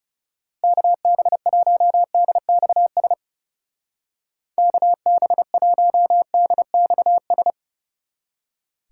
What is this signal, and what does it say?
Morse code, heard as a keyed tone, about 30 wpm, 700 Hz. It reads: KB1DXS KB1DXS
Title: Speed Racing - USA 2x3 Call Signs 30wpm